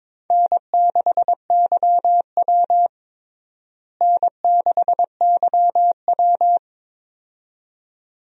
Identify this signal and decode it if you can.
Morse code, heard as a keyed tone, about 22 wpm, 700 Hz. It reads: N6YW N6YW
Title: USA Call Signs - 4 Characters - Repeated 2x - 22wpm